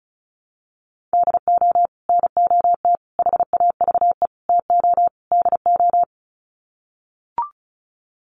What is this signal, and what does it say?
Morse code, heard as a keyed tone, about 34 wpm, 700 Hz. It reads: DO NOT HAVE TO DO
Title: Sets of 5 Words 35wpm